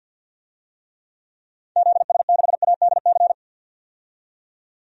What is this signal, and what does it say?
Morse code, heard as a keyed tone, about 49 wpm, 700 Hz. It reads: ZS6ADC
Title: DXCC-100 Call Signs - 6 Characters - 50wpm